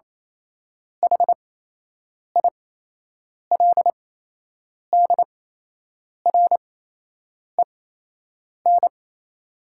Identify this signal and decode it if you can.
Morse code, heard as a keyed tone, about 28 wpm, spaced as for 8 wpm, 700 Hz. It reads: HILDREN
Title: 8x Character Spacing - Top 500 Words - 28wpm